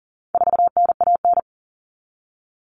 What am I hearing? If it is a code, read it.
Morse code, heard as a keyed tone, about 40 wpm, 700 Hz. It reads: 4NAN
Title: Common Cut Numbers - 4-Digit 40wpm